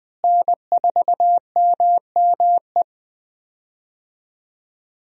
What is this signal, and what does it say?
Morse code, heard as a keyed tone, about 20 wpm, 700 Hz. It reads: N4MME
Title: USA Call Signs - 5 Characters - Repeated 3x - 20wpm